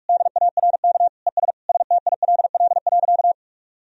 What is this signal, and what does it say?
Morse code, heard as a keyed tone, about 46 wpm, 700 Hz. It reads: DARK ES STILL.
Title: Sentences from Top 500 Words and Abbreviations 45wpm